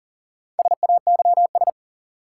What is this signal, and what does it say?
Morse code, heard as a keyed tone, about 40 wpm, 700 Hz. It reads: SAYS
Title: Sets of 2 Words - New Word Drill - 40wpm